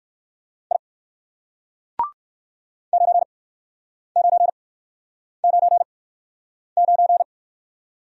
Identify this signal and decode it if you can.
Morse code, heard as a keyed tone, about 64 wpm, spaced as for 9 wpm, 700 Hz. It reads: I 9999